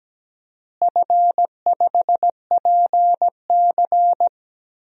Morse code, heard as a keyed tone, about 17 wpm, 700 Hz. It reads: F5PC
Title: DXCC-100 Call Signs - 4 Characters - Repeated 3x - 17wpm